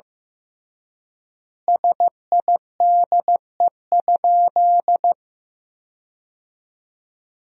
Morse code, heard as a keyed tone, about 16 wpm, 700 Hz. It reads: SIDE?
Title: 3x Word-Spacing - Sentences from Top 400 Words - 15wpm